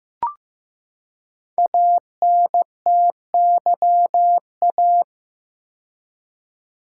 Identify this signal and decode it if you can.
Morse code, heard as a keyed tone, about 15 wpm, 700 Hz. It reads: ANTYA
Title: Code Groups - 5 Letter:Numbers - 15wpm